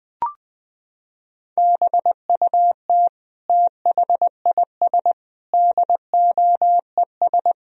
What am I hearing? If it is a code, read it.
Morse code, heard as a keyed tone, about 20 wpm, 700 Hz. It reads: BUT THIS DOES
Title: Sentences from Top 1000 Words - Review Part 2 - 20wpm